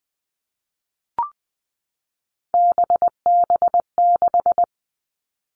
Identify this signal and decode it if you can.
Morse code, heard as a keyed tone, about 20 wpm, 700 Hz. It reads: BB6
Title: Three Challenging Letter-Number Combinations 20wpm